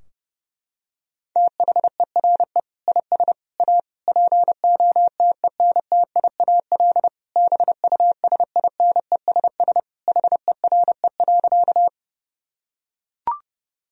Morse code, heard as a keyed tone, about 30 wpm, 700 Hz. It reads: THERE IS A POTENTIAL BUSINESS HERE.
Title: Sentences from Top 1000 Words - Review Part 4 - 30wpm